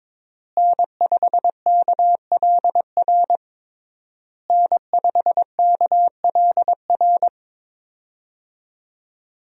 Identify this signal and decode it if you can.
Morse code, heard as a keyed tone, about 22 wpm, 700 Hz. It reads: N5KLR N5KLR
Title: USA Call Signs - 5 Characters - Repeated 3x - 22wpm